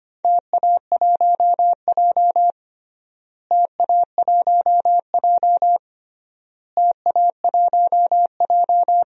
Morse code, heard as a keyed tone, about 25 wpm, 700 Hz. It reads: TA1J TA1J TA1J
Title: DXCC-100 Call Signs - 4 Characters - Repeated 3x - 25wpm